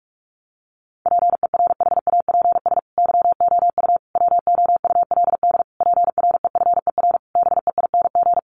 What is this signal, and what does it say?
Morse code, heard as a keyed tone, about 44 wpm, 700 Hz. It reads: PERHAPS YOU WOULD PREFER BEING
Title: Sentences from Top 1100 Words 45wpm